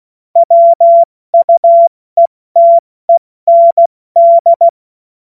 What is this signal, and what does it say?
Morse code, heard as a keyed tone, about 16 wpm, 680 Hz. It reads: WUETEND
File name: Witz-08-Juli-2025-70-F80